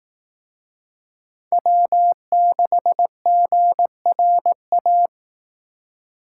Morse code, heard as a keyed tone, about 18 wpm, 700 Hz. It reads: W6GRA
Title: Speed Racing - USA 1x3 Call Signs 15wpm